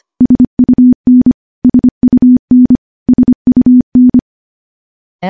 Morse code, heard as a keyed tone, about 25 wpm, 250 Hz. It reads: SUN SUN SUN
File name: morse-training-25WPM